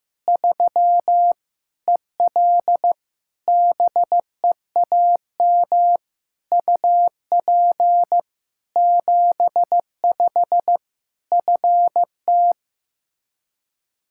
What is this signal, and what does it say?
Morse code, heard as a keyed tone, about 15 wpm, 700 Hz. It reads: 3 EL BEAM UP 75 FT
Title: K1USN SST QSO Practice 15wpm